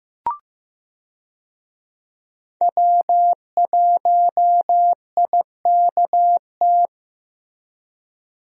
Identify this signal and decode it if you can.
Morse code, heard as a keyed tone, about 15 wpm, 700 Hz. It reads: W1IKT
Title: USA Call Signs - 5 Characters - Repeated 2x - 15wpm